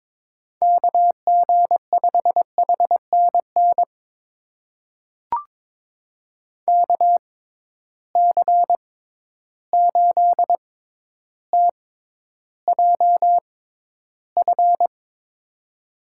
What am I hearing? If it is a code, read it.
Morse code, heard as a keyed tone, about 22 wpm, spaced as for 11 wpm, 700 Hz. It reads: KG5HNN KC8TJF
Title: USA Call Signs - 6 Characters - 6x Character Spacing - 22wpm